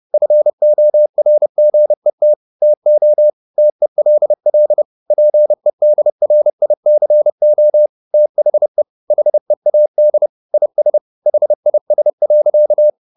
Morse code, heard as a keyed tone, about 30 wpm, 600 Hz. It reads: FORGET TO TELL PEDRICO THE HEAD IS HIS.